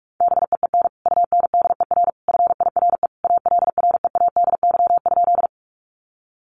Morse code, heard as a keyed tone, about 44 wpm, 700 Hz. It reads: BEEN UNDER FIRE ALREADY?
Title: Sentences from Top 700 Words 45wpm